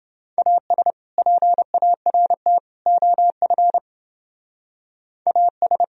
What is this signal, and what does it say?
Morse code, heard as a keyed tone, about 30 wpm, 700 Hz. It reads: AS PART OF AS